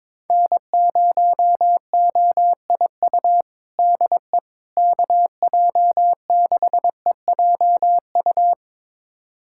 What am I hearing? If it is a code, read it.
Morse code, heard as a keyed tone, about 22 wpm, 700 Hz. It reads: N0OIU DE KJ6EJU
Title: USA Callsign DE Callsign 22wpm